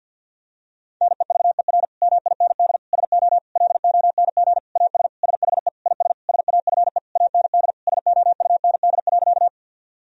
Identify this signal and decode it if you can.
Morse code, heard as a keyed tone, about 51 wpm, 700 Hz. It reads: NEVER MIND SO LONG AS SHE IS SAFE AND SOUND.
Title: Sentences from Top 1000 Words - Review Part 1 - 50wpm